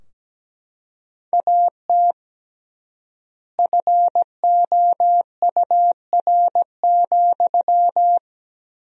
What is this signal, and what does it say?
Morse code, heard as a keyed tone, about 17 wpm, 700 Hz. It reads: AT FOUR,
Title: 3x Word-Spacing - Sentences from Top 300 Words - 17wpm